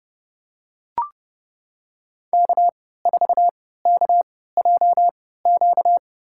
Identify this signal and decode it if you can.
Morse code, heard as a keyed tone, about 30 wpm, spaced as for 17 wpm, 700 Hz. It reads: K4KJQ
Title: USA Call Signs - 5 Characters - 3x Character Spacing - 30wpm